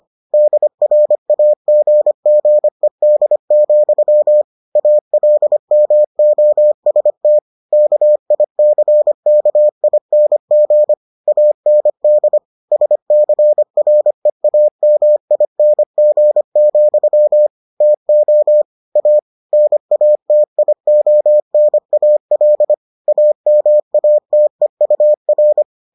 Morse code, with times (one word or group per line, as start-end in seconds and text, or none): 0.3-4.4 DRAGGED,
4.8-7.4 ALMOST
7.7-10.9 KICKING
11.3-12.4 AND
12.7-17.5 SCREAMING,
17.8-18.6 TO
19.0-19.2 A
19.5-22.7 NATIONAL
23.1-25.6 AMATEUR